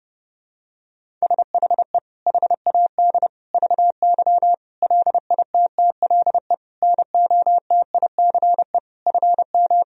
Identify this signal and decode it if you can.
Morse code, heard as a keyed tone, about 32 wpm, 700 Hz. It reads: SHE HAD VY LITTLE NOTICE FM
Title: Sentences from Top 400 Words and Abbreviations 30wpm